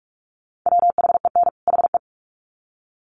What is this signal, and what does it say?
Morse code, heard as a keyed tone, about 44 wpm, 700 Hz. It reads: WHEN HE
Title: Sets of 2 Words 45wpm